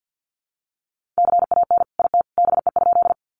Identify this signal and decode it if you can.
Morse code, heard as a keyed tone, about 50 wpm, 700 Hz. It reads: CAN IT BE?
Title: Sentences from Top 100 Words 50wpm